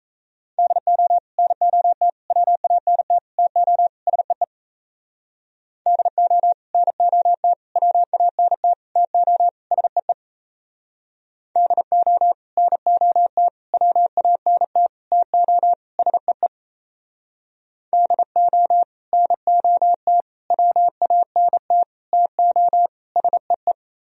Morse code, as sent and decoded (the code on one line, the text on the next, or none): -.. --- / -. --- - / .-- .- -. - / - --- / ... . / -.. --- / -. --- - / .-- .- -. - / - --- / ... . / -.. --- / -. --- - / .-- .- -. - / - --- / ... . / -.. --- / -. --- - / .-- .- -. - / - --- / ... .
DO NOT WANT TO SEE DO NOT WANT TO SEE DO NOT WANT TO SEE DO NOT WANT TO SEE